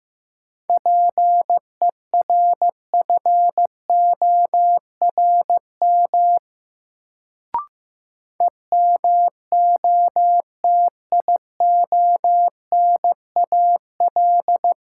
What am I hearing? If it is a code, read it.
Morse code, heard as a keyed tone, about 15 wpm, 700 Hz. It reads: PERFORM EMOTIONAL
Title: Sentences from Top 1000 Words - Review Part 5 - New Word Drill - 15wpm